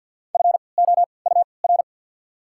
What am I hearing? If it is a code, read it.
Morse code, heard as a keyed tone, about 53 wpm, spaced as for 29 wpm, 700 Hz. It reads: FOUR
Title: 3x Character Spacing - Top 500 Words - 50wpm